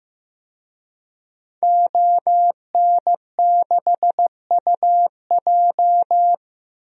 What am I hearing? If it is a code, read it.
Morse code, heard as a keyed tone, about 15 wpm, 700 Hz. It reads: ON6UJ
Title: DXCC-100 Call Signs - 5 Characters - Repeated 3x - 15wpm